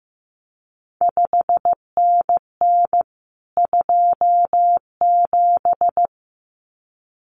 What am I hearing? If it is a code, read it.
Morse code, heard as a keyed tone, about 15 wpm, 700 Hz. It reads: 5NN 27
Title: CQ WW DX Contest Exchange Encore - 15wpm